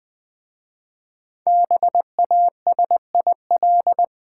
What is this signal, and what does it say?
Morse code, heard as a keyed tone, about 20 wpm, 700 Hz. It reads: BASIL
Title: Easy Words with Hints - 20wpm